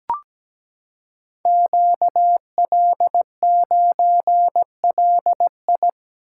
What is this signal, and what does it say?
Morse code, heard as a keyed tone, about 17 wpm, 700 Hz. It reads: QL9LI
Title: Code Groups - 5 Letter:Numbers - 17wpm